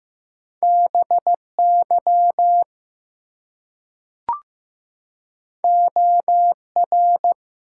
Morse code, heard as a keyed tone, about 15 wpm, 700 Hz. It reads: BY OR
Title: Two Letter Words - 15wpm